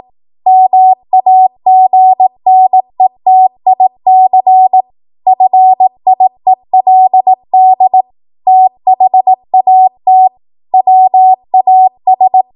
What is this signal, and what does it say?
Morse code, heard as a keyed tone, about 18 wpm, 750 Hz. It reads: MAGNETIC FIELD THAT WAS